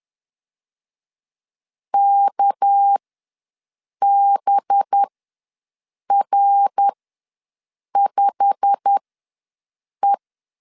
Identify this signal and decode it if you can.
Morse code, heard as a keyed tone, about 10 wpm, spaced as for 6 wpm, 800 Hz. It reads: KBR5E
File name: LEZIONI / 8 YJ45 / 8 YJ45 ws 10wpm 3843